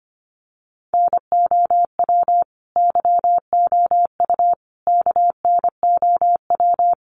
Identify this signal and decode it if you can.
Morse code, heard as a keyed tone, about 25 wpm, 700 Hz. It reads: NOW YOU KNOW
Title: Sentences from Top 1200 Words 25wpm